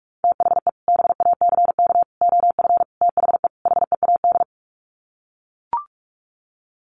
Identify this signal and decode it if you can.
Morse code, heard as a keyed tone, about 44 wpm, 700 Hz. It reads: THE BACK OF THE HEAD
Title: Sets of 5 Words Encore2 45wpm